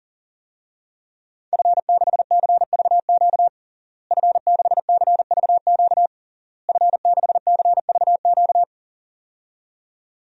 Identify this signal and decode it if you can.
Morse code, heard as a keyed tone, about 40 wpm, 700 Hz. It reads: F6CVQ F6CVQ F6CVQ
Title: DXCC-100 Call Signs - 5 Characters - Repeated 3x - 40wpm